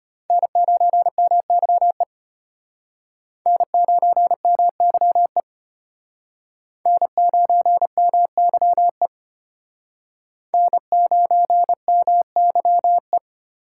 Morse code, heard as a keyed tone, about 31 wpm, 700 Hz. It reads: N9MYE N9MYE N9MYE N9MYE